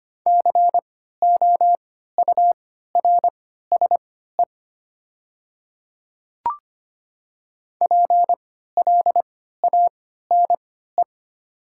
Morse code, heard as a keyed tone, about 25 wpm, spaced as for 14 wpm, 700 Hz. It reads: COURSE PLANE
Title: 3x Character Spacing - Top 500 Words - 25wpm